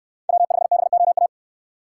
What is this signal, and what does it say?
Morse code, heard as a keyed tone, about 65 wpm, 700 Hz. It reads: W5BOA